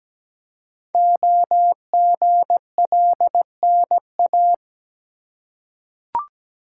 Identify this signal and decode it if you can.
Morse code, heard as a keyed tone, about 17 wpm, 700 Hz. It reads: OGLNA
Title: Code Groups - 5 Letters - 17wpm